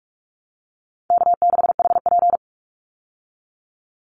Morse code, heard as a keyed tone, about 44 wpm, 700 Hz. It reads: K6HP
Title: Easy Call Signs Repeated 3x - 45wpm